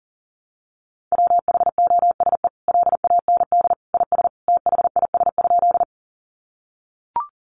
Morse code, heard as a keyed tone, about 37 wpm, 700 Hz. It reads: WHOSE LAND IS THIS?
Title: Sentences from Top 1200 Words 40wpm